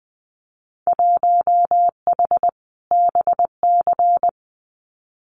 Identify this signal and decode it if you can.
Morse code, heard as a keyed tone, about 20 wpm, 700 Hz. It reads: 1H BC
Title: Winter Field Day Exchanges 20wpm